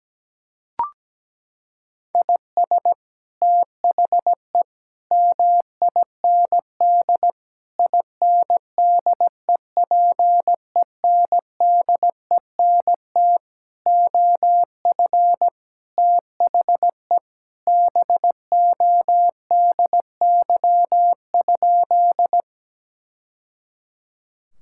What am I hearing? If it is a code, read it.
Morse code, heard as a keyed tone, about 17 wpm, 700 Hz. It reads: IS THE MIND INDEPENDENT OF THE BODY?
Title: Sentences from Top 1000 Words - Review Part 6 - 17wpm